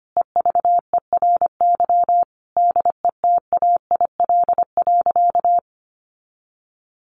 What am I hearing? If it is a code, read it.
Morse code, heard as a keyed tone, about 25 wpm, 700 Hz. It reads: EVERY DETAIL.
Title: Sentences from Top 1500 Words 25wpm